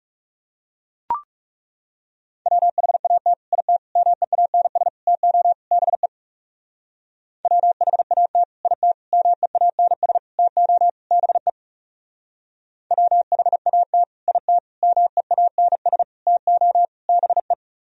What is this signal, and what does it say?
Morse code, heard as a keyed tone, about 38 wpm, 700 Hz. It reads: WHAT IT MEANS TO BE WHAT IT MEANS TO BE WHAT IT MEANS TO BE